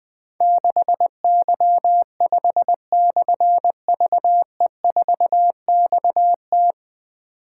Morse code, heard as a keyed tone, about 20 wpm, 700 Hz. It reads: 6Y5/VE4XT
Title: Difficult Call Signs 20wpm